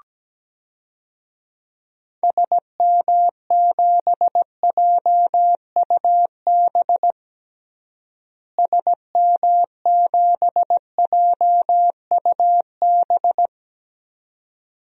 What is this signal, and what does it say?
Morse code, heard as a keyed tone, about 17 wpm, 700 Hz. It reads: SM7JUB SM7JUB
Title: DXCC-100 Call Signs - 6 Characters - Repeated 2x - 17wpm